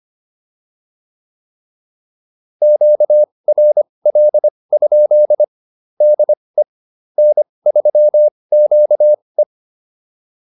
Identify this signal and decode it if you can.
Morse code, heard as a keyed tone, about 25 wpm, spaced as for 20 wpm, 600 Hz. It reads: QRL? DE N3QE